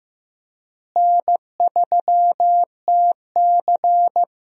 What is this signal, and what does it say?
Morse code, heard as a keyed tone, about 15 wpm, 700 Hz. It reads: N3TC
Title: USA Call Signs - 4 Characters - Repeated 3x - 15wpm